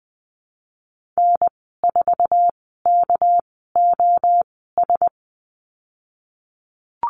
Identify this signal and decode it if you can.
Morse code, heard as a keyed tone, about 20 wpm, spaced as for 15 wpm, 700 Hz. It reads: N4KOS
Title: 238 Lesson 38g - Letter q - Call Signs 20wpm